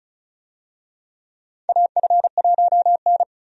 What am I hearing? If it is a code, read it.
Morse code, heard as a keyed tone, about 36 wpm, 700 Hz. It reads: AF1N
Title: USA Call Signs - 4 Characters - 35wpm